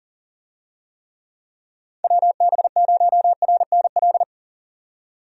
Morse code, heard as a keyed tone, about 37 wpm, 700 Hz. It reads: WB0RNL
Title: USA Call Signs - 6 Characters - 40wpm